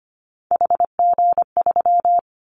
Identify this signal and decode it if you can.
Morse code, heard as a keyed tone, about 25 wpm, 700 Hz. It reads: HG3
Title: Callsign Prefixes for Top 25 DXCC - Repeated 3x - 25wpm